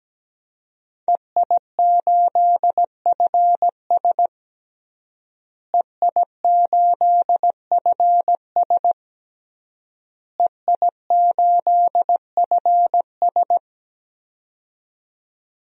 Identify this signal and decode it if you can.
Morse code, heard as a keyed tone, about 17 wpm, 700 Hz. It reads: EI8FS EI8FS EI8FS
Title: DXCC-100 Call Signs - 5 Characters - Repeated 3x - 17wpm